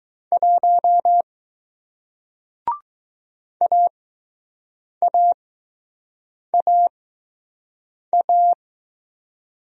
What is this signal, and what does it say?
Morse code, heard as a keyed tone, about 20 wpm, spaced as for 6 wpm, 700 Hz. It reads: 1 AAAA